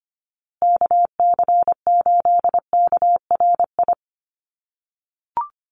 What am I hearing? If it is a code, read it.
Morse code, heard as a keyed tone, about 25 wpm, 700 Hz. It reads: KC8KRI
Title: USA Call Signs Repeated 3x 25wpm